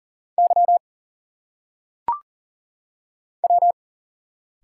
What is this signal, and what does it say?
Morse code, heard as a keyed tone, about 39 wpm, 700 Hz. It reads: Y W
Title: Single Letters - Phonetic Pronunciation 40wpm